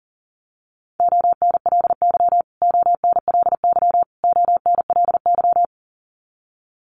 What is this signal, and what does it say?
Morse code, heard as a keyed tone, about 40 wpm, 700 Hz. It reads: ONLY ONLY ONLY
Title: Top 200 Words Repeated 40wpm